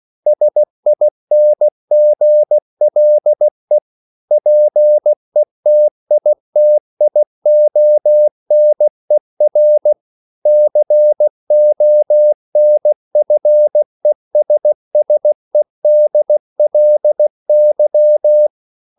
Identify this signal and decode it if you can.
Morse code, heard as a keyed tone, about 16 wpm, 600 Hz. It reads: SINGLE PETITIONER CONFESSEDLY
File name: CW / Practice / EnglishQuotations0416